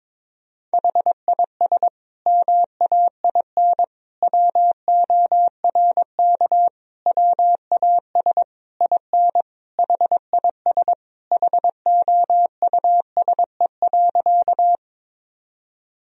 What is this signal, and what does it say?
Morse code, heard as a keyed tone, about 22 wpm, 700 Hz. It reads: HIS MAIN WORK WAS IN HIS HOUSE.
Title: Sentences from Top 1000 Words - Review Part 2 - 22wpm